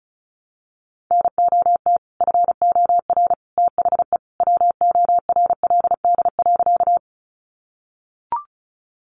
Morse code, heard as a keyed tone, about 34 wpm, 700 Hz. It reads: NOT FOR THE WORLD.
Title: Sentences from Top 600 Words 35wpm